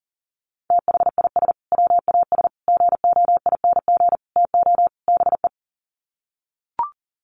Morse code, heard as a keyed tone, about 40 wpm, 700 Hz. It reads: THIS WAS GOING TO BE
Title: Sets of 5 Words Encore2 40wpm